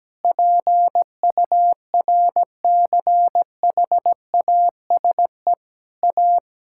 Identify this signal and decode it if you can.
Morse code, heard as a keyed tone, about 17 wpm, 700 Hz. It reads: PURCHASE A
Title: Sentences from Top 1000 Words - Review Part 6 - 17wpm